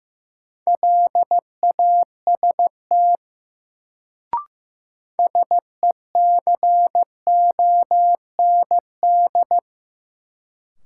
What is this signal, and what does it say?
Morse code, heard as a keyed tone, about 15 wpm, 700 Hz. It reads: LAST SECOND
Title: Sets of 2 Words - New Word Drill - 15wpm